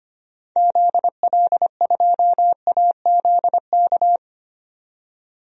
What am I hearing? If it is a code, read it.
Morse code, heard as a keyed tone, about 25 wpm, 700 Hz. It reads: ZL2AZK
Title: DXCC-100 Call Signs - 6 Characters - 25wpm